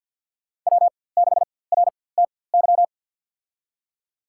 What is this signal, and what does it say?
Morse code, heard as a keyed tone, about 52 wpm, spaced as for 23 wpm, 700 Hz. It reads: W6RTY
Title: USA Call Signs - 5 Characters - 4x Character Spacing - 50wpm